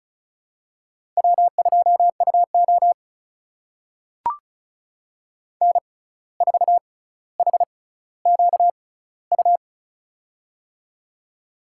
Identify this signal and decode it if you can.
Morse code, heard as a keyed tone, about 36 wpm, spaced as for 17 wpm, 700 Hz. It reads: W2UO N4HQU
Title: USA Call Signs - 6x Character Spacing - 35wpm